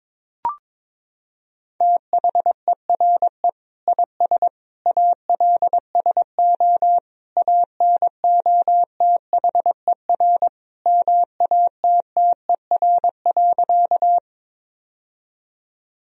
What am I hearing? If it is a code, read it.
Morse code, heard as a keyed tone, about 22 wpm, 700 Hz. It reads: THERE IS ALSO ANOTHER MATTER.
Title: Sentences from Top 1000 Words - Review Part 1 - 22wpm